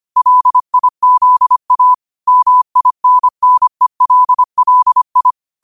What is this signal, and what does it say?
Morse code, heard as a keyed tone, about 25 wpm, 1000 Hz. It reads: LIZA MINNELLI
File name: qotdfiles / 25wpm